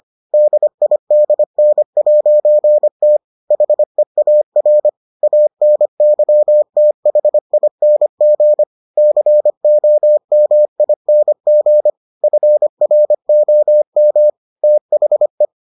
Morse code, heard as a keyed tone, about 25 wpm, 600 Hz. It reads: DIDN'T HEAR ANYTHING COMING FROM THE